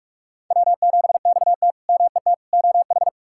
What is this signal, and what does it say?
Morse code, heard as a keyed tone, about 46 wpm, 700 Hz. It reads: W7XT MET OH
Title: K1USN SST QSO Practice - 45wpm